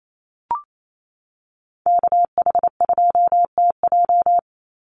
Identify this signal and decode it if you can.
Morse code, heard as a keyed tone, about 28 wpm, 700 Hz. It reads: KH2TJ
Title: NA SOTA Chaser Callsigns 28wpm